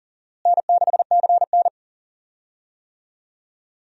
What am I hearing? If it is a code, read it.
Morse code, heard as a keyed tone, about 39 wpm, 700 Hz. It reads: N6CN
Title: USA Call Signs - 4 Characters - Repeated 3x - 40wpm